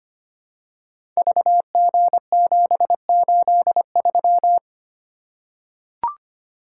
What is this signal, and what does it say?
Morse code, heard as a keyed tone, about 25 wpm, 700 Hz. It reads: VG783
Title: Code Groups - 5 Letter:Numbers - 25wpm